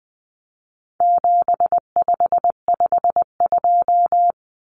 Morse code, heard as a keyed tone, about 20 wpm, 700 Hz. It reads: 7552
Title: Common Cut Numbers - 4-Digit 20wpm